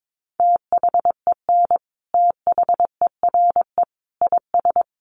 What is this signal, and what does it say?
Morse code, heard as a keyed tone, about 22 wpm, 700 Hz. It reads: THEN THERE IS
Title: Sentences from Top 100 Words 22wpm